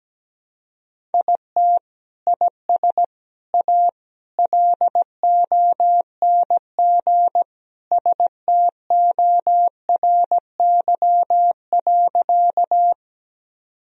Morse code, heard as a keyed tone, about 17 wpm, 700 Hz. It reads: IT IS A LONG STORY.
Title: Sentences from Top 1000 Words - Review Part 1 - 17wpm